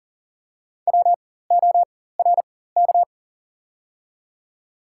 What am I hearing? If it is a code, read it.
Morse code, heard as a keyed tone, about 42 wpm, spaced as for 19 wpm, 700 Hz. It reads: WORK
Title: 4x Character Spacing - Top 200 Words - 40wpm